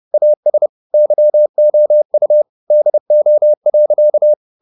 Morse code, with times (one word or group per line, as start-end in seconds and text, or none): 0.1-0.7 AS
0.9-2.4 YOU
2.7-4.3 DO.